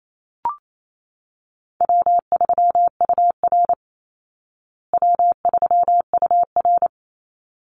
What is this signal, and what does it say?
Morse code, heard as a keyed tone, about 28 wpm, 700 Hz. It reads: W3UR W3UR
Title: Easy Call Signs Repeated 2x - 28wpm